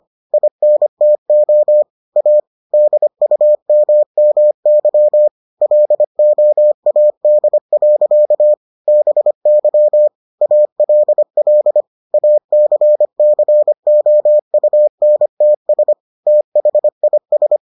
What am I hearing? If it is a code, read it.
Morse code, heard as a keyed tone, about 25 wpm, 600 Hz. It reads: INTO A DUMMY LOAD. BY ALL ACCOUNTS THIS